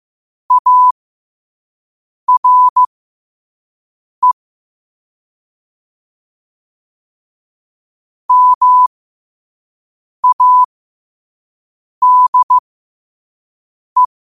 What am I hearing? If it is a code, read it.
Morse code, heard as a keyed tone, about 15 wpm, spaced as for 5 wpm, 1000 Hz. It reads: ARE MADE
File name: qotdfiles / 5wpm